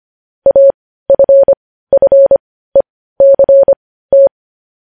Morse code, heard as a keyed tone, about 25 wpm, spaced as for 15 wpm, 550 Hz. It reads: AFFECT
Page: news headlines in morse code at 15 wpm